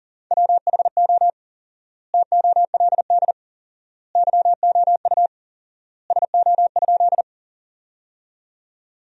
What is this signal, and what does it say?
Morse code, heard as a keyed tone, about 39 wpm, 700 Hz. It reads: WHO TOLD YOU SO?
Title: 4x Word-Spacing - Sentences from Top 400 Words - 40wpm